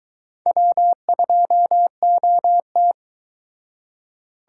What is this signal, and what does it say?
Morse code, heard as a keyed tone, about 23 wpm, 700 Hz. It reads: W2OT